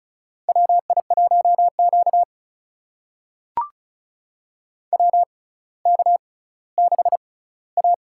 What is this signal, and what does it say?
Morse code, heard as a keyed tone, about 36 wpm, spaced as for 18 wpm, 700 Hz. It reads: WI1Q WK6A
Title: USA Call Signs - 4 Characters - 6x Character Spacing - 35wpm